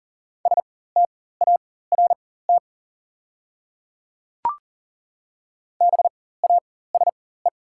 Morse code, heard as a keyed tone, about 40 wpm, spaced as for 19 wpm, 690 Hz. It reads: START BASE
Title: 4x Character Spacing - Top 300 Words - 40wpm